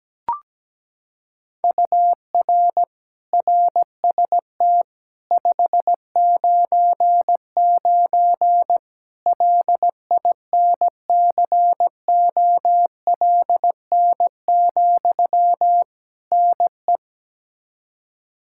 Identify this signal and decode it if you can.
Morse code, heard as a keyed tone, about 17 wpm, 700 Hz. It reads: UR RST 599 LINCOLN, NE
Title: Practice QSO - Exchange 1 - Beginning - 17wpm